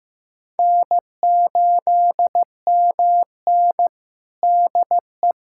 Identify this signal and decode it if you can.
Morse code, heard as a keyed tone, about 15 wpm, 700 Hz. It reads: N8MN DE
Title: USA Callsign DE Callsign 15wpm